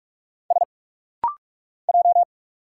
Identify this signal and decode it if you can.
Morse code, heard as a keyed tone, about 46 wpm, 700 Hz. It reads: S J
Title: Single Letters - Rapid-Fire 45wpm